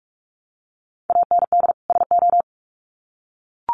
Morse code, heard as a keyed tone, about 44 wpm, 700 Hz. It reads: AND SO
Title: Sets of 2 Words 45wpm